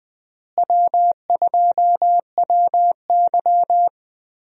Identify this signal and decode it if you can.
Morse code, heard as a keyed tone, about 20 wpm, 700 Hz. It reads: W2WY
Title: USA Call Signs - 4 Characters - Repeated 3x - 20wpm